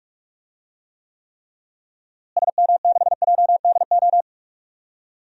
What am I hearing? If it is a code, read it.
Morse code, heard as a keyed tone, about 46 wpm, 700 Hz. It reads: SM6JDO
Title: DXCC-100 Call Signs - 6 Characters - Repeated 3x - 45wpm